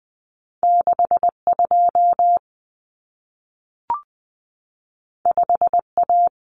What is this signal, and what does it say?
Morse code, heard as a keyed tone, about 20 wpm, 700 Hz. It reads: 62 5A
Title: Common Cut Numbers - 2-Digit 20wpm